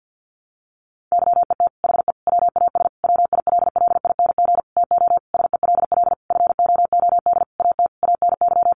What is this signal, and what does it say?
Morse code, heard as a keyed tone, about 50 wpm, 700 Hz. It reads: YET HE WAS WILLING TO SELL FOOD AT ANY
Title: Sentences from Top 1100 Words 50wpm